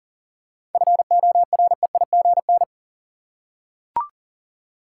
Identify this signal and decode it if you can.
Morse code, heard as a keyed tone, about 40 wpm, 700 Hz. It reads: FOREIGN
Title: Seven Letter Words - 40wpm